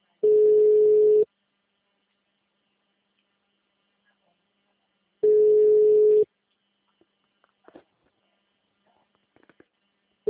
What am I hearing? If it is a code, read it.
Ringback tone (Europe, 425 Hz).
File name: out- -2026- - - .54693